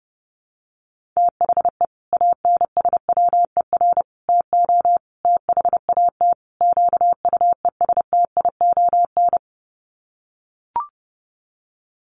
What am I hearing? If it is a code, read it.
Morse code, heard as a keyed tone, about 30 wpm, 700 Hz. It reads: THE ANSWER TO THAT QUESTION
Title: Sets of 5 Words Encore2 30wpm